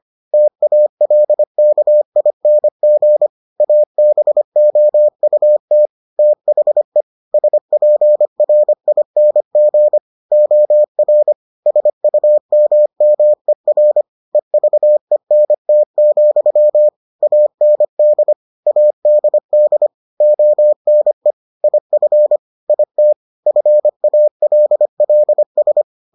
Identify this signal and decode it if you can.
Morse code, heard as a keyed tone, about 25 wpm, 600 Hz. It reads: TALKING ABOUT THE SPRING OR SUMMER EVENT, AND ADD ONE IF IT FALLS